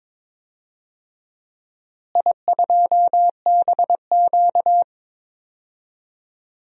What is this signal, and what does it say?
Morse code, heard as a keyed tone, about 22 wpm, 700 Hz. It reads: I2BQ
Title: DXCC-100 Call Signs - 4 Characters - Repeated 3x - 22wpm